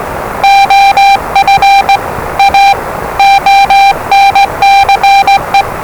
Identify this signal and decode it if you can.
Morse code, heard as a keyed tone, about 20 wpm, 790 Hz, with noise. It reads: OF A ONCE
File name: WellingtonRadioZlwCloseDownOnHfAndMf / 01-Zlw 4277khz 417.5 qrt until 0925z